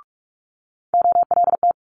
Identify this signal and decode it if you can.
Morse code, heard as a keyed tone, about 46 wpm, 700 Hz. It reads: OLT
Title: Multi-letter Phonemes Quick Fire 45wpm